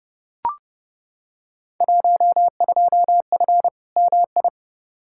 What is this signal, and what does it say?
Morse code, heard as a keyed tone, about 30 wpm, 700 Hz. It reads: 12F MI
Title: Field Day Exchanges Encore - 30wpm